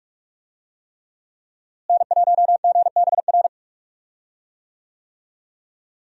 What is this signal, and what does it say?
Morse code, heard as a keyed tone, about 46 wpm, 700 Hz. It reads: N1GBR
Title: USA Call Signs - 5 Characters - Repeated 3x - 45wpm